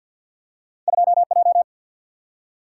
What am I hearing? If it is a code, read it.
Morse code, heard as a keyed tone, about 53 wpm, 700 Hz. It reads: 2J
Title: Single-Letter Single-Number 50wpm